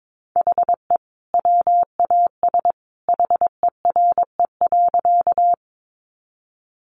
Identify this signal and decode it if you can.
Morse code, heard as a keyed tone, about 22 wpm, 700 Hz. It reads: HE WAS HERE.
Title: Sentences from Top 200 Words 22wpm